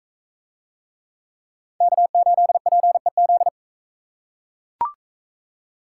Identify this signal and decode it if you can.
Morse code, heard as a keyed tone, about 40 wpm, 700 Hz. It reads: K8PEZ